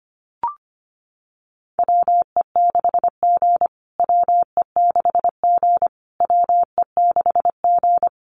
Morse code, heard as a keyed tone, about 25 wpm, 700 Hz. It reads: WE6G WE6G WE6G
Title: USA Call Signs Repeated 3x 25wpm